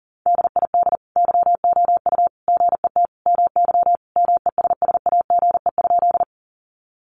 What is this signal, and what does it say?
Morse code, heard as a keyed tone, about 40 wpm, 700 Hz. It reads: DID YOU GET MY MESSAGE?
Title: Sentences from Top 1100 Words 40wpm